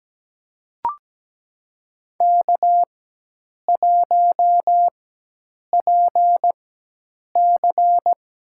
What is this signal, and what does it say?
Morse code, heard as a keyed tone, about 17 wpm, spaced as for 8 wpm, 700 Hz. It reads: K1PC
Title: USA Call Signs - 4x Character Spacing - 17wpm